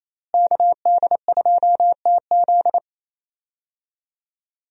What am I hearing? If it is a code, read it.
Morse code, heard as a keyed tone, about 28 wpm, 700 Hz. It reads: KD2TZ
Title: USA Call Signs - 5 Characters - Repeated 3x - 28wpm